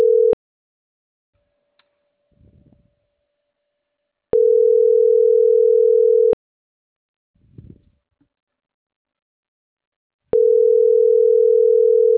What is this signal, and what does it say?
Ringback tone (North America), 440 + 480 Hz.